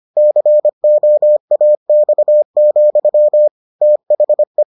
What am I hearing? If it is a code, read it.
Morse code, heard as a keyed tone, about 25 wpm, 600 Hz. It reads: COAX, THE